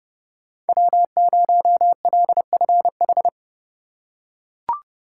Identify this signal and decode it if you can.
Morse code, heard as a keyed tone, about 31 wpm, 700 Hz. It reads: W0LFH